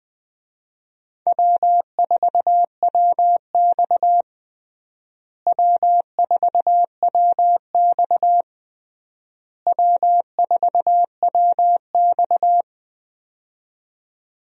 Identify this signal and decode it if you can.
Morse code, heard as a keyed tone, about 20 wpm, 700 Hz. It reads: W4WX W4WX W4WX
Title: USA Call Signs - 4 Characters - Repeated 3x - 20wpm